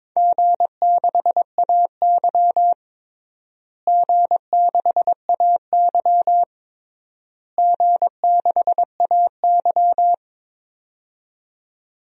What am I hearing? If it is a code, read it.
Morse code, heard as a keyed tone, about 22 wpm, 700 Hz. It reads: G6AY G6AY G6AY
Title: DXCC-100 Call Signs - 4 Characters - Repeated 3x - 22wpm